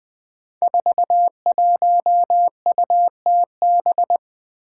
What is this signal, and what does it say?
Morse code, heard as a keyed tone, about 20 wpm, 700 Hz. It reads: 41UTB
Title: Five Challenging Letter-Number Combinations 20wpm